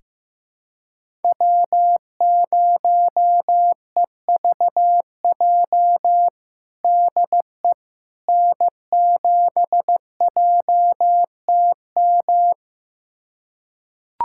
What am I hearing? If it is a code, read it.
Morse code, heard as a keyed tone, about 16 wpm, 700 Hz. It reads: W0EVJ DE N7JTM
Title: USA Callsign DE Callsign - 5 Characters - 15wpm